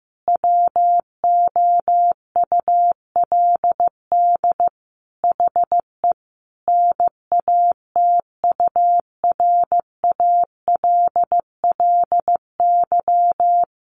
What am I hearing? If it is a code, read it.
Morse code, heard as a keyed tone, about 15 wpm, 700 Hz. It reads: WOULD HE NATURALLY
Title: Sentences from Top 1300 Words 15wpm